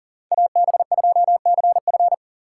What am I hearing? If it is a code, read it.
Morse code, heard as a keyed tone, about 38 wpm, 700 Hz. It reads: AB2CF
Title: USA Call Signs - 5 Characters - 4x Character Spacing - 40wpm